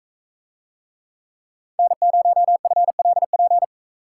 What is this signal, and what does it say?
Morse code, heard as a keyed tone, about 42 wpm, 700 Hz. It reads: N0FLP